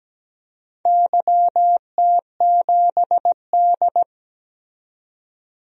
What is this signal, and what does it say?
Morse code, heard as a keyed tone, about 17 wpm, 700 Hz. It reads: YT7D
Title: DXCC-100 Call Signs - 4 Characters - 17wpm